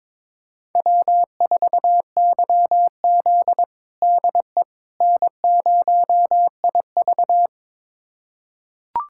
Morse code, heard as a keyed tone, about 22 wpm, 700 Hz. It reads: W4YZ DE N0IV
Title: USA Callsign DE Callsign - 4 Characters - 22wpm